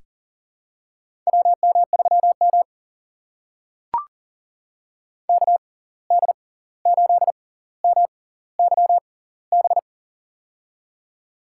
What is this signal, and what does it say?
Morse code, heard as a keyed tone, about 40 wpm, spaced as for 18 wpm, 700 Hz. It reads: WM3M KD8MYB
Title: USA Call Signs - 6x Character Spacing - 40wpm